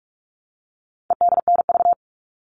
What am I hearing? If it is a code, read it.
Morse code, heard as a keyed tone, about 44 wpm, 700 Hz. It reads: EDNV
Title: All Cut Numbers - 4-Digit 45wpm